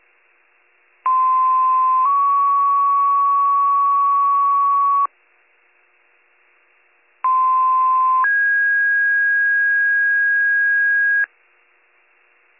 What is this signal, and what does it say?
Two-tone page, heard as a one-second tone, then a three-second tone, about 1040 then 1120 Hz, about 1040 then 1650 Hz. It is heard from 1.1 to 11.3 s.